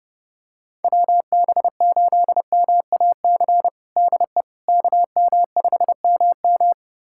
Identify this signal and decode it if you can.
Morse code, heard as a keyed tone, about 32 wpm, 700 Hz. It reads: WB8MAC DE KM5MM
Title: USA Callsign DE Callsign 30wpm